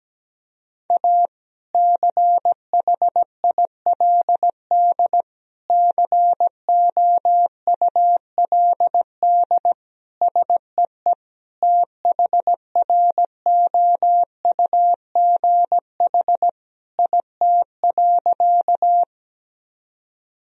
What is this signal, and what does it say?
Morse code, heard as a keyed tone, about 17 wpm, 700 Hz. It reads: A CHILD COULD SEE THROUGH IT.
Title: Sentences from Top 1000 Words - Review Part 1 - 17wpm